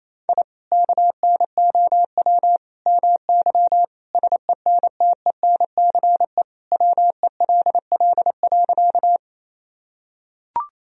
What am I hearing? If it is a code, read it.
Morse code, heard as a keyed tone, about 28 wpm, 700 Hz. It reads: I KNOW MY SENTENCE WELL.
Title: Sentences from Top 1000 Words - Review Part 1 - 28wpm